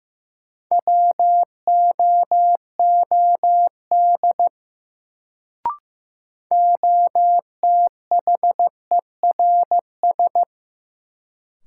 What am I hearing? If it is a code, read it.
Morse code, heard as a keyed tone, about 15 wpm, 700 Hz. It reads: WOOD OTHERS
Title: Sentences from Top 1000 Words - Review Part 2 - New Word Drill - 15wpm